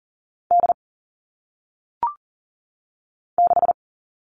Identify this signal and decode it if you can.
Morse code, heard as a keyed tone, about 39 wpm, 700 Hz. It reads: D 6
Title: All Cut Numbers - 1-Digit 40wpm